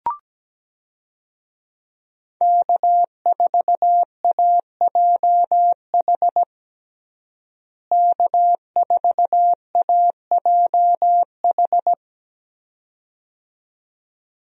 Morse code, heard as a keyed tone, about 17 wpm, 700 Hz. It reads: K4AJH K4AJH
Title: USA Call Signs - 5 Characters - Repeated 2x - 17wpm